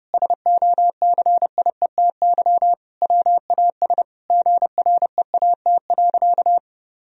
Morse code, heard as a keyed tone, about 30 wpm, 700 Hz. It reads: SOCIETY WAS GREAT.
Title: Sentences from Top 1000 Words - Review Part 4 - 30wpm